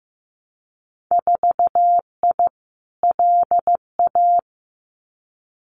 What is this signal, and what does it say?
Morse code, heard as a keyed tone, about 15 wpm, 700 Hz. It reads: 4I LA
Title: Winter Field Day Exchanges Encore 15wpm